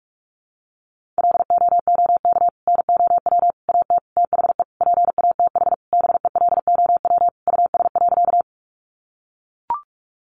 Morse code, heard as a keyed tone, about 44 wpm, 700 Hz. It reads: LOOK NOW AT THE PATH BELOW US.
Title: Sentences from Top 1400 Words 45wpm